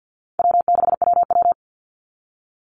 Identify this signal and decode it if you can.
Morse code, heard as a keyed tone, about 49 wpm, 700 Hz. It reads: W6WW
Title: NA SOTA Chaser Callsigns 50wpm